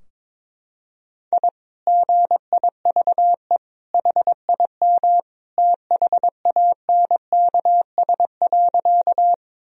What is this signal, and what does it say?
Morse code, heard as a keyed tone, about 22 wpm, 700 Hz. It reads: I GIVE HIM THANKS.
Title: Sentences from Top 1000 Words - Review Part 3 - 22wpm